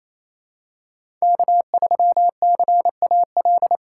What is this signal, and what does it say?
Morse code, heard as a keyed tone, about 28 wpm, 700 Hz. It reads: K3CAL
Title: USA Call Signs - 3x Character Spacing - 28wpm